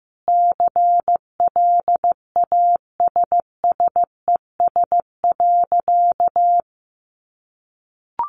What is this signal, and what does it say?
Morse code, heard as a keyed tone, about 15 wpm, 700 Hz. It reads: CLASSES.
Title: Additional Easy Phrases 15wpm